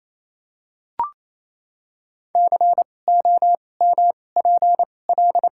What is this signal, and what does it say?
Morse code, heard as a keyed tone, about 28 wpm, spaced as for 20 wpm, 700 Hz. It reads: COMPL
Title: 2x Character Spacing - Top 500 Words - 28wpm